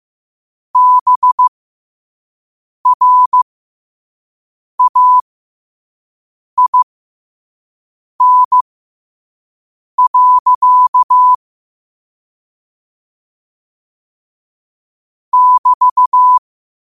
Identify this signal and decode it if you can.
Morse code, heard as a keyed tone, about 15 wpm, spaced as for 5 wpm, 1000 Hz. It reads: BRAIN. <BT>
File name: qotdfiles / 5wpm